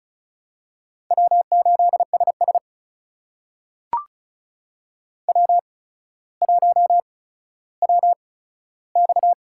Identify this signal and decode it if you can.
Morse code, heard as a keyed tone, about 36 wpm, spaced as for 15 wpm, 700 Hz. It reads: W8SS W1WX
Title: USA Call Signs - 4 Characters - 8x Character Spacing - 35wpm